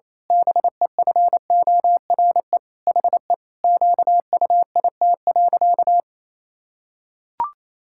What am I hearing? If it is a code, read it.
Morse code, heard as a keyed tone, about 28 wpm, 700 Hz. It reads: BEFORE HE QUIT.
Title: Pangram Challenge v2 - 28wpm